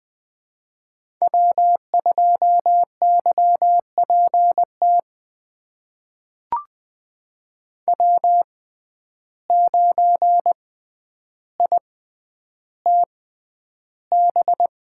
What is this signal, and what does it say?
Morse code, heard as a keyed tone, about 20 wpm, spaced as for 10 wpm, 700 Hz. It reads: W2YPT W9ITB
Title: USA Call Signs - 6x Character Spacing - 20wpm